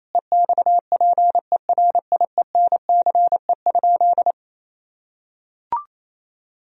Morse code, heard as a keyed tone, about 28 wpm, 700 Hz. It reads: EXPERIENCE?
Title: Sentences from Top 1000 Words - Review Part 4 - 28wpm